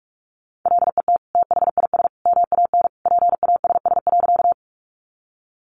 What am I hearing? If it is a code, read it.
Morse code, heard as a keyed tone, about 44 wpm, 700 Hz. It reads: LET THIS MAN PASS.
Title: Sentences from Top 400 Words 45wpm